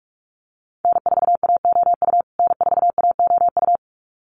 Morse code, heard as a keyed tone, about 44 wpm, 700 Hz. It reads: N4AOU N4AOU
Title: USA Call Signs Repeated 2x 45wpm